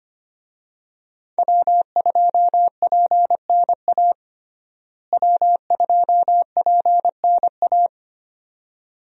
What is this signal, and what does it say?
Morse code, heard as a keyed tone, about 25 wpm, 700 Hz. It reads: W2PNA W2PNA
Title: USA Call Signs - 5 Characters - Repeated 2x - 25wpm